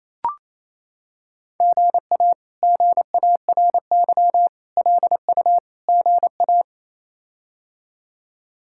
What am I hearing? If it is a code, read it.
Morse code, heard as a keyed tone, about 27 wpm, 700 Hz. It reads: GA GARY LU GA
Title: K1USN SST QSO Practice - 28wpm